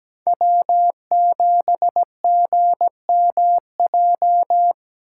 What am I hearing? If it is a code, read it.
Morse code, heard as a keyed tone, about 17 wpm, 700 Hz. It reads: W7GMJ
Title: USA Call Signs - 5 Characters - 2x Character Spacing - 17wpm